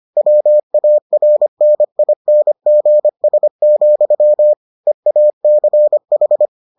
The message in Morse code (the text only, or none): WARNINGS, EACH